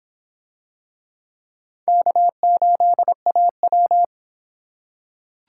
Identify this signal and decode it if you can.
Morse code, heard as a keyed tone, about 24 wpm, 700 Hz. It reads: K8AW